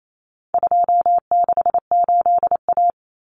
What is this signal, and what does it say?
Morse code, heard as a keyed tone, about 28 wpm, 700 Hz. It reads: 268A
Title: Common Cut Numbers - 4-Digit 28wpm